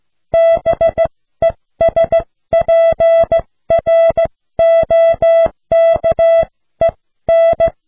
Morse code, heard as a keyed tone, about 15 wpm, spaced as for 13 wpm, 660 Hz. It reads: BESPROKEN